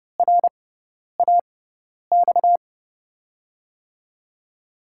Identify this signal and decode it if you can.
Morse code, heard as a keyed tone, about 32 wpm, spaced as for 10 wpm, 700 Hz. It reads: RAX
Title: USA Call Signs - 6 Characters - 6x Character Spacing - 30wpm